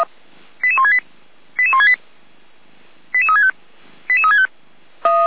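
Five-tone selective call (ZVEI), heard as quick runs of five tones, tones of about 70 ms. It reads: 88168 88168 88365 88365
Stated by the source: ZVEI five-tone selective call received by SDR, Orlamünde fire brigade (feuerwehr-orlamuende).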